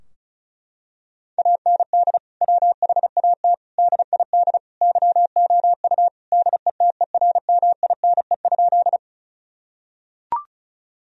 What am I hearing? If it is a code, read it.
Morse code, heard as a keyed tone, about 34 wpm, 700 Hz. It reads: AND WHAT DID YOU DETERMINE?
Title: Sentences from Top 1000 Words - Review Part 4 - 35wpm